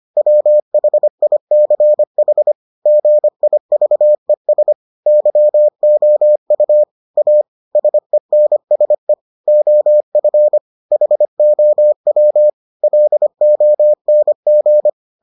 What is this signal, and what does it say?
Morse code, heard as a keyed tone, about 25 wpm, 600 Hz. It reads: WHICH GIVES YOU A SENSE OF HOW LONG